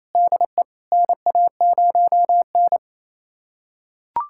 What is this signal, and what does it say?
Morse code, heard as a keyed tone, about 28 wpm, 700 Hz. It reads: DE NA0N
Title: USA Callsign DE Callsign - 4 Characters - 28wpm